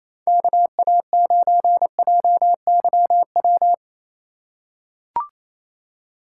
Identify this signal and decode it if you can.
Morse code, heard as a keyed tone, about 28 wpm, 700 Hz. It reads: KA9JYW
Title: USA Call Signs - 4x Character Spacing - 28wpm